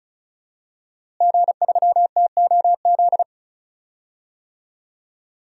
Morse code, heard as a keyed tone, about 36 wpm, 700 Hz. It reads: G3TOZ
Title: DXCC-100 Call Signs - 5 Characters - Repeated 3x - 35wpm